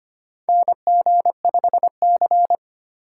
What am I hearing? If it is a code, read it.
Morse code, heard as a keyed tone, about 25 wpm, 700 Hz. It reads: NG5C
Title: USA Call Signs - 4 Characters - 4x Character Spacing - 25wpm